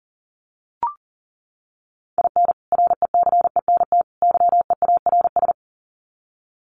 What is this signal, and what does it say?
Morse code, heard as a keyed tone, about 38 wpm, 700 Hz. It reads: IN RECENT YEARS
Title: Sets of 3 Words 40wpm